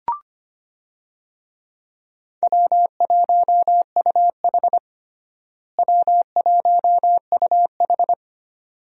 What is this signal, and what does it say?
Morse code, heard as a keyed tone, about 25 wpm, 700 Hz. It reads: W1UH W1UH
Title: USA Call Signs - 4 Characters - Repeated 2x - 25wpm